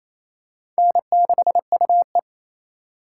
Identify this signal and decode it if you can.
Morse code, heard as a keyed tone, about 28 wpm, 700 Hz. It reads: N6UE
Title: USA Call Signs - 4 Characters - 4x Character Spacing - 28wpm